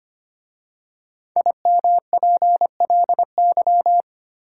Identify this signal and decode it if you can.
Morse code, heard as a keyed tone, about 25 wpm, 700 Hz. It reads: IMPLY
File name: 5 Letter Words - First 3 Letters Given - 25wpm